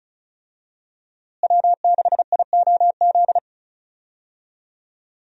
Morse code, heard as a keyed tone, about 35 wpm, 700 Hz. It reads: W6IOZ
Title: USA Call Signs - 5 Characters - Repeated 2x - 35wpm